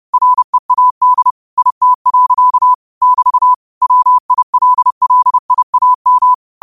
Morse code, heard as a keyed tone, about 30 wpm, 1000 Hz. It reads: READ IT. <BT> WILLIAM
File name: qotdfiles / 30wpm